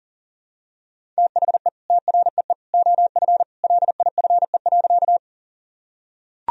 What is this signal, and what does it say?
Morse code, heard as a keyed tone, about 40 wpm, 700 Hz. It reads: THE TREE OF LIFE.
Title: Sentences from Top 1000 Words - Review Part 1 - 40wpm